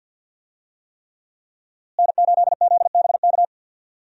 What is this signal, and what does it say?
Morse code, heard as a keyed tone, about 52 wpm, 700 Hz. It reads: N8ZBK
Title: USA Call Signs - 5 Characters - Repeated 3x - 50wpm